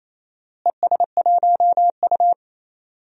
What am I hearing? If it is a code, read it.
Morse code, heard as a keyed tone, about 28 wpm, 700 Hz. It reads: ES1U
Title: DXCC-100 Call Signs - 4 Characters - Repeated 2x - 28wpm